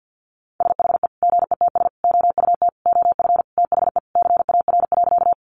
Morse code, heard as a keyed tone, about 50 wpm, 700 Hz. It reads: SHE GETS OUT OF THE CAR.
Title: Easy Phrases 50wpm